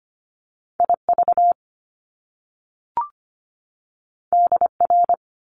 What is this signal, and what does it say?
Morse code, heard as a keyed tone, about 25 wpm, 700 Hz. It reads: IV DR
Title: Top Two Letter Combinations in Words 25wpm